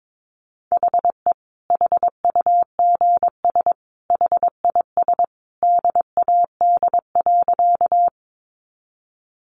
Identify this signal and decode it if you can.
Morse code, heard as a keyed tone, about 22 wpm, 700 Hz. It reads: HE HUGS HIS DAD.
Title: Easy Phrases 22wpm